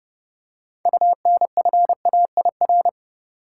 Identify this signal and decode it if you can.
Morse code, heard as a keyed tone, about 32 wpm, 700 Hz. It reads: UNFAIR
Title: Six Letter Words - 30wpm